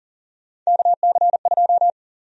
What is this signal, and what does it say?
Morse code, heard as a keyed tone, about 40 wpm, 700 Hz. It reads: KC2
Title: USA Call Sign Prefixes - Repeated 2x - 40wpm